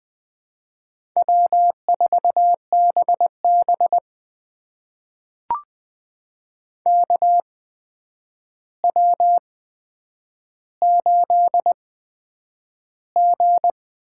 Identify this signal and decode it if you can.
Morse code, heard as a keyed tone, about 20 wpm, spaced as for 9 wpm, 700 Hz. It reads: W4BB KW8G
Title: USA Call Signs - 4 Characters - 8x Character Spacing - 20wpm